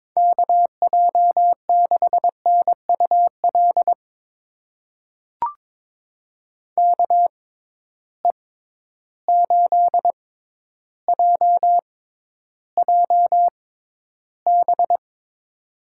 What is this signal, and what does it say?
Morse code, heard as a keyed tone, about 22 wpm, spaced as for 11 wpm, 700 Hz. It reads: KJ6NUL KE8JJB
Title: USA Call Signs - 6 Characters - 6x Character Spacing - 22wpm